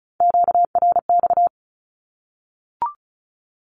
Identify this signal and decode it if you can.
Morse code, heard as a keyed tone, about 34 wpm, 700 Hz. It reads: QRX
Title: Q-Codes Repeated 3x 35wpm